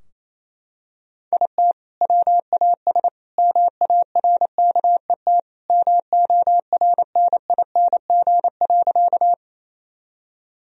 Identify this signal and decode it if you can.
Morse code, heard as a keyed tone, about 27 wpm, 700 Hz. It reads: IT WAS MARKET MORNING.
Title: Sentences from Top 1000 Words - Review Part 3 - 28wpm